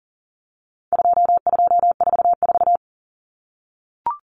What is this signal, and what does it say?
Morse code, heard as a keyed tone, about 40 wpm, 700 Hz. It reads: 2244
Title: Common Cut Numbers - 4-Digit 40wpm